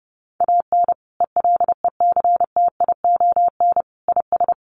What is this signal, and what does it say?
Morse code, heard as a keyed tone, about 30 wpm, 700 Hz. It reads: AN ELECTION IS
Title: Sentences from Top 1500 Words 30wpm